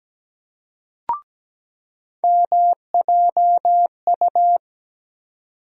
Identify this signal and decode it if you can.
Morse code, heard as a keyed tone, about 17 wpm, 700 Hz. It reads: MJU
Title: USA Call Sign Postfixes - 3 Letters 17wpm